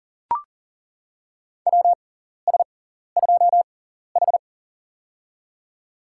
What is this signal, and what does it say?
Morse code, heard as a keyed tone, about 40 wpm, spaced as for 14 wpm, 700 Hz. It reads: WS2H
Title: USA Call Signs - 4 Characters - 6x Character Spacing - 40wpm